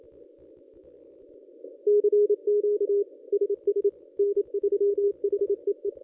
Morse code, heard as a keyed tone, about 23 wpm, 410 Hz. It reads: CQ SS N3HEE
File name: contests / ss-cw-k5tr-op-n5rz -1100